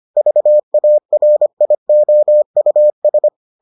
Morse code, heard as a keyed tone, about 25 wpm, 600 Hz. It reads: VARIOUS